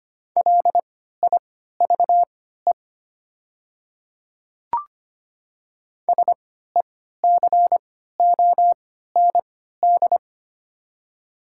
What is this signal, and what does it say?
Morse code, heard as a keyed tone, about 25 wpm, spaced as for 14 wpm, 700 Hz. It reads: LIVE SECOND
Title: 3x Character Spacing - Top 300 Words - 25wpm